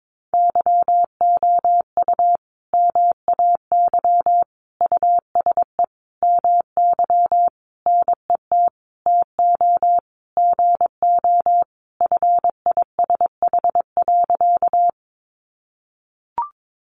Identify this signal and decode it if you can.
Morse code, heard as a keyed tone, about 22 wpm, 700 Hz. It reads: YOU MAY USE MY NET TO GO FISH.
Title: Sentences from Top 1400 Words 22wpm